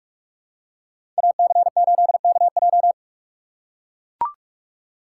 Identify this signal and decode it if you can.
Morse code, heard as a keyed tone, about 44 wpm, 700 Hz. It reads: AC8KJ
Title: USA Call Signs - 5 Characters - 2x Character Spacing - 45wpm